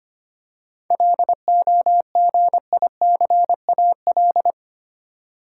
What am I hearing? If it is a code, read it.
Morse code, heard as a keyed tone, about 25 wpm, 700 Hz. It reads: LOGICAL
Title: Seven Letter Words - 25wpm